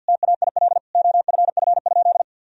Morse code, heard as a keyed tone, about 52 wpm, 700 Hz. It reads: TAIL OFF?
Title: Sentences from Top 1000 Words - Review Part 3 - 50wpm